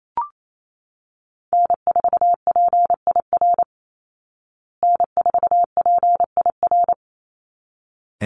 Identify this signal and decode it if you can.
Morse code, heard as a keyed tone, about 28 wpm, 700 Hz. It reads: N4PIR N4PIR
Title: Call Signs Repeated 3x - 28wpm